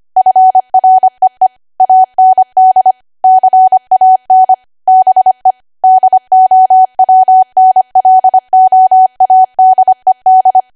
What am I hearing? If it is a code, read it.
Morse code, heard as a keyed tone, about 25 wpm, 750 Hz. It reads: FREE AND CAN BE DOWNLOADED
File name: Archive / 25 WPM / 25WPM